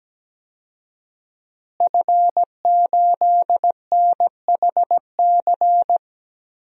Morse code, heard as a keyed tone, about 17 wpm, 700 Hz. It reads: F8NHC
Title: DXCC-100 Call Signs - 5 Characters - Repeated 3x - 17wpm